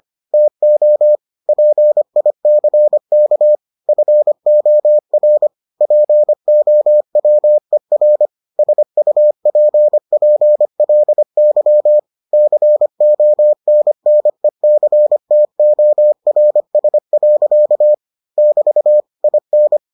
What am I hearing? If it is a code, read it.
Morse code, heard as a keyed tone, about 25 wpm, 600 Hz. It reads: TO PICK FOR POWER SUPPLY CONNECTORS. <BT> IN